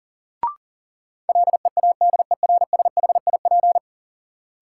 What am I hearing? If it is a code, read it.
Morse code, heard as a keyed tone, about 40 wpm, 700 Hz. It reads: LEADERSHIP
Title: Sentences from Top 1600 Words - New Word Drill - 40wpm